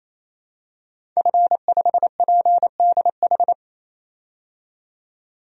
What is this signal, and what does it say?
Morse code, heard as a keyed tone, about 28 wpm, 700 Hz. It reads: F5PDH
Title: DXCC-100 Call Signs - 5 Characters - 28wpm